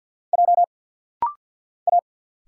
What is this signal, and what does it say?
Morse code, heard as a keyed tone, about 50 wpm, 700 Hz. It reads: J A
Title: Single Letter-Number Alt - Rapid-Fire 50wpm